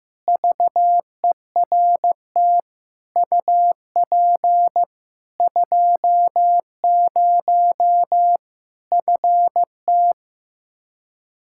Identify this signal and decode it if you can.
Morse code, heard as a keyed tone, about 15 wpm, 700 Hz. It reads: VERT UP 20 FT
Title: Practice QSO - Exchange 2 - Info - 15wpm